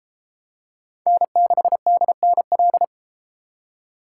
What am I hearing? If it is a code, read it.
Morse code, heard as a keyed tone, about 31 wpm, 700 Hz. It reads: N6DNL